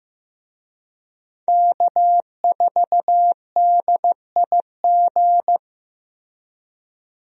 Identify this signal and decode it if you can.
Morse code, heard as a keyed tone, about 15 wpm, 700 Hz. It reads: K4DIG
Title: USA Call Signs - 5 Characters - Repeated 2x - 15wpm